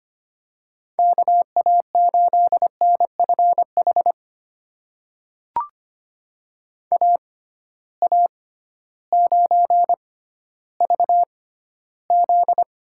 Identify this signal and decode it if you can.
Morse code, heard as a keyed tone, about 25 wpm, spaced as for 14 wpm, 700 Hz. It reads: KA8NFH AA9VZ
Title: USA Call Signs - 6x Character Spacing - 25wpm